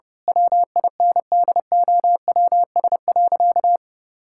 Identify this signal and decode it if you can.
Morse code, heard as a keyed tone, about 30 wpm, 700 Hz. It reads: WINDOWS.
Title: Pangram Challenge v2 - 30wpm